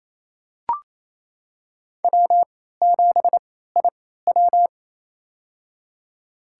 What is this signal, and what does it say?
Morse code, heard as a keyed tone, about 28 wpm, spaced as for 16 wpm, 700 Hz. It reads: W7IW
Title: USA Call Signs - 3x Character Spacing - 28wpm